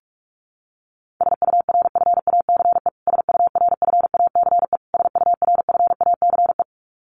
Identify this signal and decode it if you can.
Morse code, heard as a keyed tone, about 44 wpm, 700 Hz. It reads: SURFACE SURFACE SURFACE
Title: Top 500 Words Repeated 45wpm